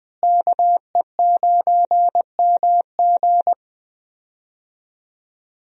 Morse code, heard as a keyed tone, about 20 wpm, 700 Hz. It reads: KE9MG
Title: USA Call Signs - 5 Characters - Repeated 3x - 20wpm